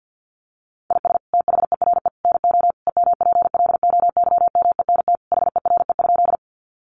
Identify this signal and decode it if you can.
Morse code, heard as a keyed tone, about 50 wpm, 700 Hz. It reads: IS THERE NO EMPLOYMENT HERE?
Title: Sentences from Top 1500 Words 50wpm